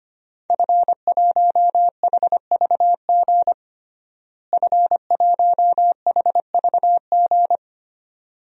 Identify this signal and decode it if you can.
Morse code, heard as a keyed tone, about 25 wpm, 700 Hz. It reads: F1HVG F1HVG
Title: DXCC-100 Call Signs - 5 Characters - Repeated 2x - 25wpm